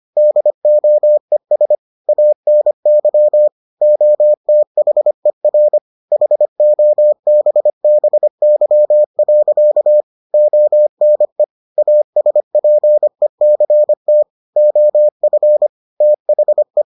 Morse code, heard as a keyed tone, about 25 wpm, 600 Hz. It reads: DOES ANY OTHER HOBBY. ONE ASPECT OF THE